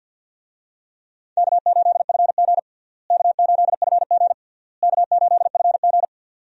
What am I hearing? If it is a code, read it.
Morse code, heard as a keyed tone, about 51 wpm, 700 Hz. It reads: K8FG K8FG K8FG
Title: USA Call Signs - 4 Characters - Repeated 3x - 50wpm